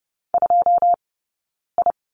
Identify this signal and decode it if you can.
Morse code, heard as a keyed tone, about 30 wpm, 700 Hz. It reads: 2 I
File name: 092 Lesson 16d - Letter c - Two Characters 30wpm